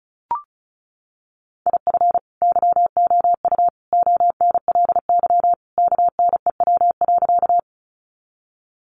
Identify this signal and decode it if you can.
Morse code, heard as a keyed tone, about 34 wpm, 700 Hz. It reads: IF YOU ONLY KNEW.
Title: Sentences from Top 400 Words 35wpm